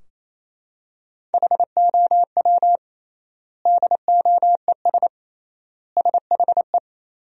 Morse code, heard as a keyed tone, about 28 wpm, 700 Hz. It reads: HOW DOES SHE
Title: 3x Word-Spacing - Sentences from Top 200 Words - 28wpm